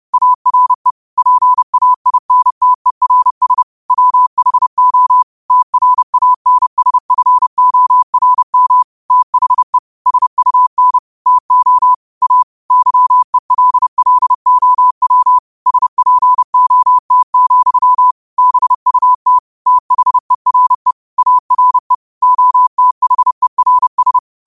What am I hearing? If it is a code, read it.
Morse code, heard as a keyed tone, about 30 wpm, 1000 Hz. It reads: ARE PAINTERS WHO TRANSFORM THE SUN TO A YELLOW SPOT, BUT THERE ARE OTHERS
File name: qotdfiles / 30wpm